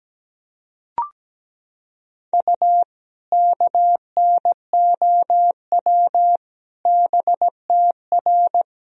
Morse code, heard as a keyed tone, about 17 wpm, 700 Hz. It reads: U KNOW BTR
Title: Sentences from Top 400 Words and Abbreviations 17wpm